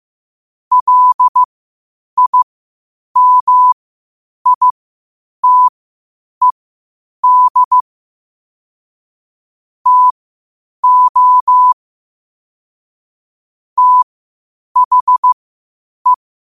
Morse code, heard as a keyed tone, about 15 wpm, spaced as for 8 wpm, 1000 Hz. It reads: LIMITED TO THE
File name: qotdfiles / 8wpm